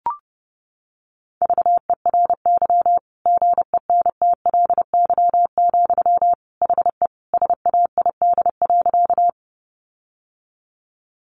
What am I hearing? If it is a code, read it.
Morse code, heard as a keyed tone, about 30 wpm, 700 Hz. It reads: VERY GENTLY, HE SAID.
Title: Sentences from Top 1500 Words 30wpm